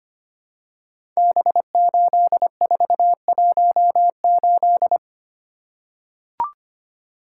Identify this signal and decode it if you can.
Morse code, heard as a keyed tone, about 25 wpm, 700 Hz. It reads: B8418
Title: Code Groups - 5 Letter:Numbers - 25wpm